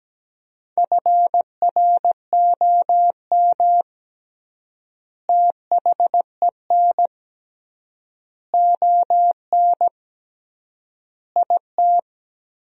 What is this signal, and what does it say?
Morse code, heard as a keyed tone, about 17 wpm, 700 Hz. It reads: FROM THEN ON IT
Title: 3x Word-Spacing - Sentences from Top 400 Words - 17wpm